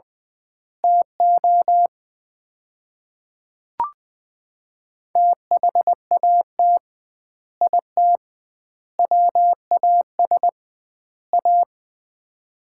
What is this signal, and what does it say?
Morse code, heard as a keyed tone, about 20 wpm, 700 Hz. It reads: TO THAT IT WAS A
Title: 2x Word-Spacing - Sets of 4 Words - 20wpm